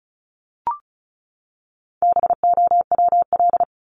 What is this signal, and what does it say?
Morse code, heard as a keyed tone, about 34 wpm, 700 Hz. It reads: BOWL
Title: Top 1001-1250 Words 35wpm